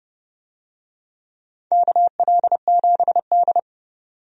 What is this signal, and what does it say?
Morse code, heard as a keyed tone, about 31 wpm, 700 Hz. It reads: KL7D
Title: DXCC-100 Call Signs - 4 Characters - 30wpm